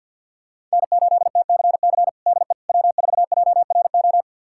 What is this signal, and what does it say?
Morse code, heard as a keyed tone, about 52 wpm, 700 Hz. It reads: N8TCC DE W4JRO
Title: USA Callsign DE Callsign - 5 Characters - 50wpm